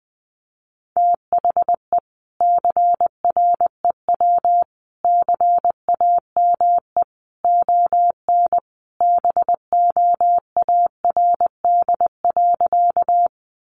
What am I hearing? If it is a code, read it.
Morse code, heard as a keyed tone, about 20 wpm, 700 Hz. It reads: THE CREW CAME ON BOARD.
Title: Sentences from Top 1500 Words 20wpm